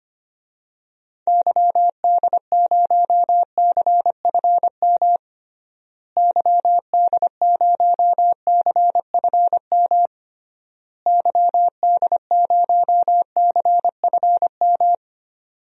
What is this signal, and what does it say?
Morse code, heard as a keyed tone, about 25 wpm, 700 Hz. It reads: YD0CFM YD0CFM YD0CFM
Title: DXCC-100 Call Signs - 6 Characters - Repeated 3x - 25wpm